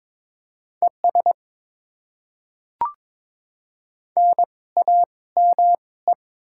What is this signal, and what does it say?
Morse code, heard as a keyed tone, about 21 wpm, spaced as for 17 wpm, 700 Hz. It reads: ES NAME
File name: Top 20 QSO Elements - 2x Character Spacing - 22wpm